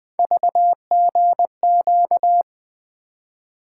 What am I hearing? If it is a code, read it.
Morse code, heard as a keyed tone, about 20 wpm, 700 Hz. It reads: VGQ
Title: Three Letter-Number Combinations - 20wpm